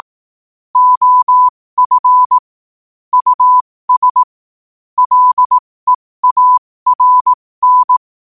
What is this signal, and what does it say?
Morse code, heard as a keyed tone, about 18 wpm, spaced as for 15 wpm, 1000 Hz. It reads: OF US LEARN
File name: qotdfiles / 15wpm